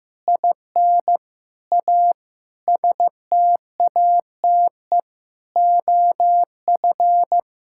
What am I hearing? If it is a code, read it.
Morse code, heard as a keyed tone, about 15 wpm, 700 Hz. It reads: IN A STATE OF
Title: Sentences from Top 1600 Words 15wpm